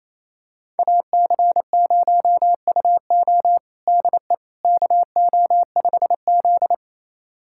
Morse code, heard as a keyed tone, about 28 wpm, 700 Hz. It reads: AC0UO DE KO5Z
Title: USA Callsign DE Callsign 28wpm